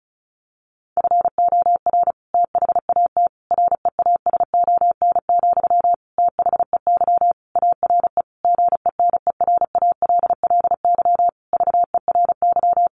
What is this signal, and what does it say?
Morse code, heard as a keyed tone, about 34 wpm, 700 Hz. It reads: FOR THAT REASON, THEY ARE GENERALLY VERY
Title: Sentences from Top 1500 Words 35wpm